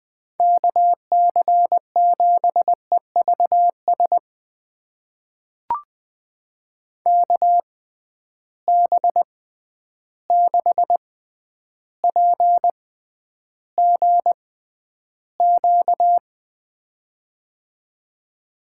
Morse code, heard as a keyed tone, about 20 wpm, spaced as for 10 wpm, 700 Hz. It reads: KC7EVS KB6PGQ
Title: USA Call Signs - 6 Characters - 6x Character Spacing - 20wpm